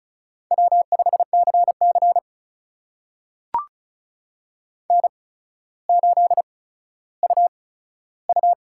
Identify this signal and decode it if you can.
Morse code, heard as a keyed tone, about 36 wpm, spaced as for 15 wpm, 700 Hz. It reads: W5CC N8UU
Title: USA Call Signs - 8x Character Spacing - 35wpm